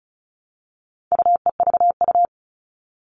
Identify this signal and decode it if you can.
Morse code, heard as a keyed tone, about 34 wpm, 700 Hz. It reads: UEVU
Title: All Cut Numbers - 4-Digit 35wpm